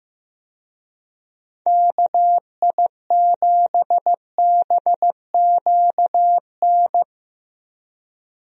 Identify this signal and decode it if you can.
Morse code, heard as a keyed tone, about 15 wpm, 700 Hz. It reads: KI7BQN
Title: USA Call Signs - 6 Characters - Repeated 3x - 15wpm